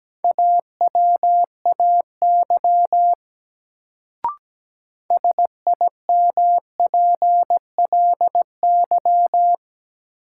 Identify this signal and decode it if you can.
Morse code, heard as a keyed tone, about 17 wpm, 700 Hz. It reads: AWAY SIMPLY
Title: Sentences from Top 600 Words - New Word Drill - 17wpm